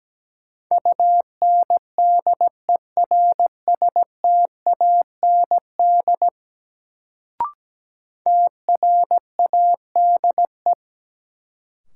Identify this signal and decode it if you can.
Morse code, heard as a keyed tone, about 17 wpm, 700 Hz. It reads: UNDERSTAND TRADE
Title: Sentences from Top 600 Words - New Word Drill - 17wpm